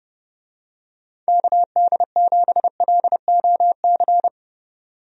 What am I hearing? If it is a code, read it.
Morse code, heard as a keyed tone, about 30 wpm, 700 Hz. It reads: KD7LOC
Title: USA Call Signs - 6 Characters - 30wpm